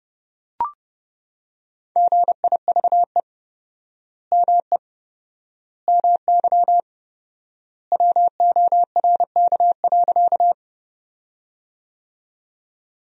Morse code, heard as a keyed tone, about 30 wpm, 700 Hz. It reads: GIVE ME MY WORK.
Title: 4x Word-Spacing - Sentences from Top 200 Words - 30wpm